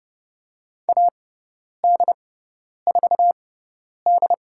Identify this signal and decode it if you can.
Morse code, heard as a keyed tone, about 30 wpm, spaced as for 10 wpm, 700 Hz. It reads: AD4D
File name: Lesson 17g - Letter u - Call Signs 30wpm